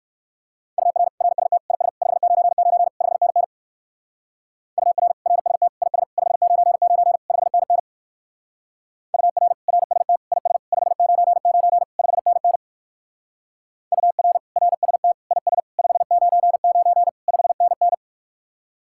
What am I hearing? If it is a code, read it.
Morse code, heard as a keyed tone, about 53 wpm, 700 Hz. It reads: UR RST IS 599 5NN UR RST IS 599 5NN UR RST IS 599 5NN UR RST IS 599 5NN
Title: Speed-Racing Common QSO Phrases - 45wpm